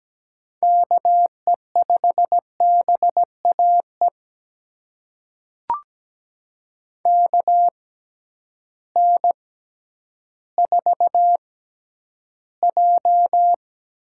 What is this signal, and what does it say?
Morse code, heard as a keyed tone, about 17 wpm, spaced as for 10 wpm, 700 Hz. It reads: KE5BAE KN4J
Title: USA Call Signs - 6 Characters - 6x Character Spacing - 17wpm